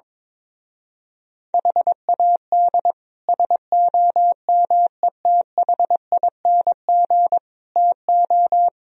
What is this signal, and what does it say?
Morse code, heard as a keyed tone, about 22 wpm, 700 Hz. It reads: HAD SOMETHING TO
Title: Speed-Racing - Sets of 5 Words 22wpm